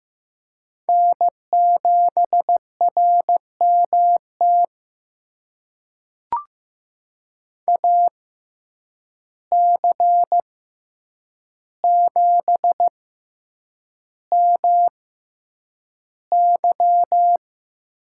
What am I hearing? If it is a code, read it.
Morse code, heard as a keyed tone, about 15 wpm, spaced as for 8 wpm, 700 Hz. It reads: N7RMT AC7MY
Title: USA Call Signs - 5 Characters - 6x Character Spacing - 15wpm